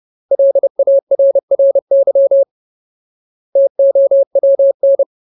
Morse code, heard as a keyed tone, about 30 wpm, 550 Hz. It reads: LARRY TOWN
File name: Ultimate Word List / 30WPM Ultimate